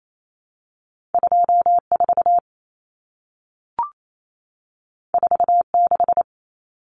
Morse code, heard as a keyed tone, about 28 wpm, 700 Hz. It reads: 24 46
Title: Numbers - 2-Digit 28wpm